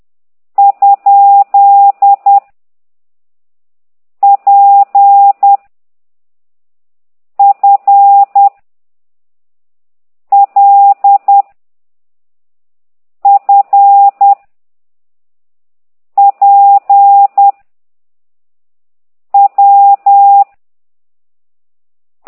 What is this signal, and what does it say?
Morse code, heard as a keyed tone, about 10 wpm, 800 Hz. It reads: ? P F L F P W
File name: ct2idw / exerc8